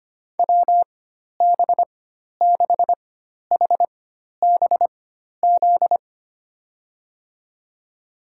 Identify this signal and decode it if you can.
Morse code, heard as a keyed tone, about 25 wpm, spaced as for 12 wpm, 700 Hz. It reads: WB6HBZ
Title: USA Call Signs - 6 Characters - 4x Character Spacing - 25wpm